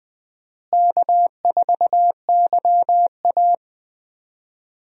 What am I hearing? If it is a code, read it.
Morse code, heard as a keyed tone, about 20 wpm, 700 Hz. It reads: K4YA
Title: USA Call Signs - 4 Characters - Repeated 2x - 20wpm